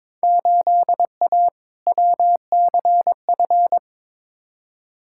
Morse code, heard as a keyed tone, about 22 wpm, 700 Hz. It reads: 8A WCF
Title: Field Day Exchanges Encore - 22wpm